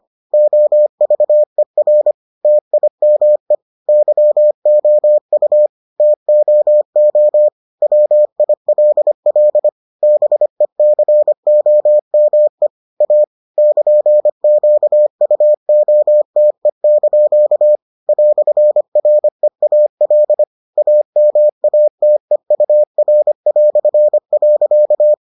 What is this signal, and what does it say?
Morse code, heard as a keyed tone, about 25 wpm, 600 Hz. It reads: OVER TIME YOU TOO WILL BECOME A <KN>QUOTE) "REAL AMATEUR".